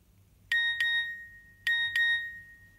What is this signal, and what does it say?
There is no code.